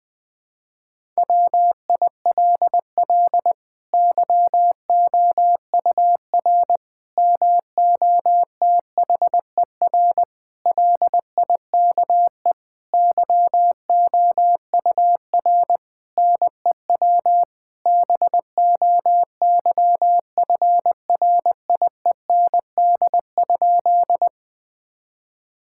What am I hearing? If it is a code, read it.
Morse code, heard as a keyed tone, about 20 wpm, 700 Hz. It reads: WILL YOUR MOTHER LIKE YOUR NEW BOYFRIEND?
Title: Sentences from Top 1000 Words - Review Part 5 - 20wpm